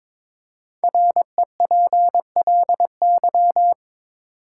Morse code, heard as a keyed tone, about 22 wpm, 700 Hz. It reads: REPLY
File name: 5 Letter Words - First 3 Letters Given - 22wpm